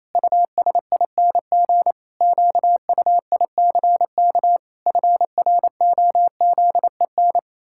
Morse code, heard as a keyed tone, about 28 wpm, 700 Hz. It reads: USING QUICK FROZEN
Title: Pangram Challenge v2 - 28wpm